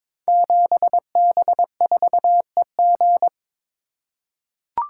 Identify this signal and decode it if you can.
Morse code, heard as a keyed tone, about 22 wpm, 700 Hz. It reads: 7B4EG
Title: Code Groups - 5 Letter:Numbers - 22wpm